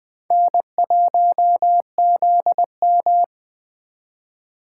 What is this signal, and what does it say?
Morse code, heard as a keyed tone, about 20 wpm, 700 Hz. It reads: N1ZM
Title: Speed Racing - USA 1x2 Call Signs 17wpm